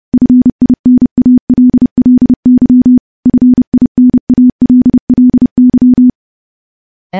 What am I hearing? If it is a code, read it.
Morse code, heard as a keyed tone, about 32 wpm, 250 Hz. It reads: FINALLY FINALLY
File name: morse-training-30WPM